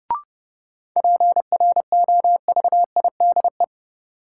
Morse code, heard as a keyed tone, about 30 wpm, 700 Hz. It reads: PROVIDE
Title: Sentences from Top 1000 Words - Review Part 2 - New Word Drill - 30wpm